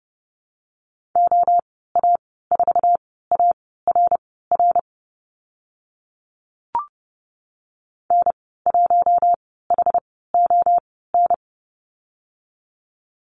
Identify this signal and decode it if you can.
Morse code, heard as a keyed tone, about 30 wpm, spaced as for 17 wpm, 700 Hz. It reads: OA4ARR N1HON
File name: Lesson 14g - Number 2 - Call Signs 30wpm